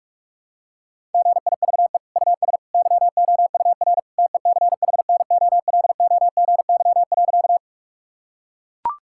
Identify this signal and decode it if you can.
Morse code, heard as a keyed tone, about 41 wpm, 700 Hz. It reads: GIVE US YOUR TECHNOLOGY.
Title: Sentences from Top 1000 Words - Review Part 5 - 45wpm